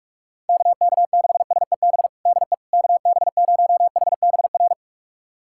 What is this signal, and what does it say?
Morse code, heard as a keyed tone, about 41 wpm, 700 Hz. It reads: KK6SEB DE KB0HBR
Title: USA Callsign DE Callsign - 6 Characters - 45wpm